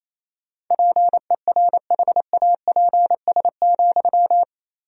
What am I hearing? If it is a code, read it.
Morse code, heard as a keyed tone, about 28 wpm, 700 Hz. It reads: PERHAPS,
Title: 3x Word-Spacing - Sentences from Top 500 Words - 28wpm